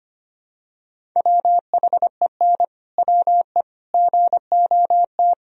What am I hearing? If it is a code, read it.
Morse code, heard as a keyed tone, about 25 wpm, 700 Hz. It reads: WHEN WE GOT
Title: Sentences from Top 500 Words and Abbreviations 25wpm